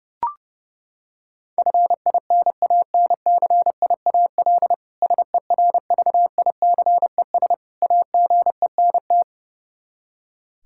Morse code, heard as a keyed tone, about 30 wpm, 700 Hz. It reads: FINANCIAL SERVICES AGENT
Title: K1USN SST QSO Practice 30wpm